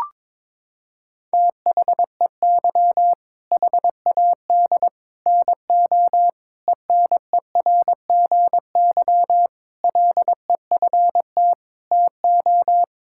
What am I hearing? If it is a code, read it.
Morse code, heard as a keyed tone, about 22 wpm, 700 Hz. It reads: THEY HAD NO ENERGY LEFT TO
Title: Sentences from Top 1000 Words - Review Part 6 - 22wpm